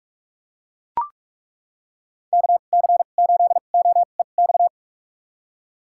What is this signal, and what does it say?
Morse code, heard as a keyed tone, about 42 wpm, spaced as for 31 wpm, 700 Hz. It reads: KC8OEX
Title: USA Call Signs - 2x Character Spacing - 45wpm